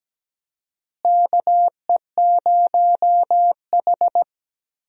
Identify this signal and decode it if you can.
Morse code, heard as a keyed tone, about 17 wpm, 700 Hz. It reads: KE0H
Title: USA Call Signs - 4 Characters - Repeated 2x - 17wpm